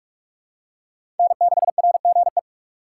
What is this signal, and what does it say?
Morse code, heard as a keyed tone, about 42 wpm, 700 Hz. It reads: N6RGE
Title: USA Call Signs - 5 Characters - 2x Character Spacing - 45wpm